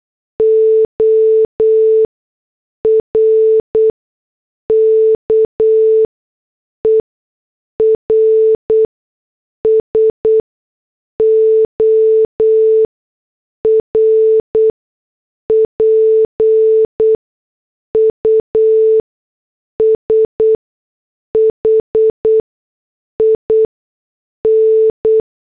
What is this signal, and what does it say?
Morse code, heard as a keyed tone, about 8 wpm, spaced as for 6 wpm, 440 Hz. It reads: ORKERS OR PUSHIN